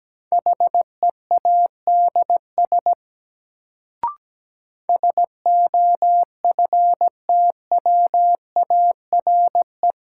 Morse code, heard as a keyed tone, about 17 wpm, 700 Hz. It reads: HEADS SOFTWARE
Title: Sentences from Top 1000 Words - Review Part 3 - New Word Drill - 17wpm